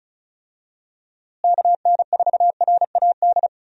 Morse code, heard as a keyed tone, about 36 wpm, 700 Hz. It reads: KN4RAD
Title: USA Call Signs - 6 Characters - Repeated 2x - 35wpm